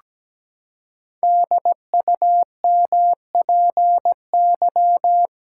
Morse code, heard as a keyed tone, about 17 wpm, 700 Hz. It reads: DUMPY
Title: Pangram Challenge v2 - 17wpm